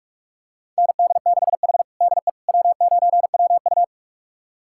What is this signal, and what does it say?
Morse code, heard as a keyed tone, about 42 wpm, 700 Hz. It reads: ND6H DE W9WU
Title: USA Callsign DE Callsign - 4 Characters - 45wpm